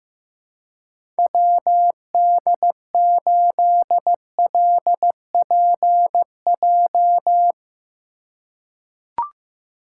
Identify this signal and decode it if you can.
Morse code, heard as a keyed tone, about 15 wpm, 700 Hz. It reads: WD8LPJ
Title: USA Call Signs - 6 Characters - 2x Character Spacing - 15wpm